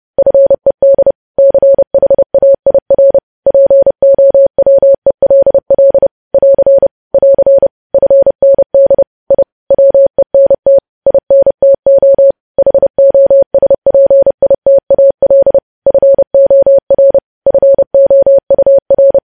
Morse code, heard as a keyed tone, about 30 wpm, 550 Hz. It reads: FED CHAIR POWELL <AR> <AR> FND I WENT INTO HOSPITAL FOR FOUR
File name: podcast / News headlines-1x-30x30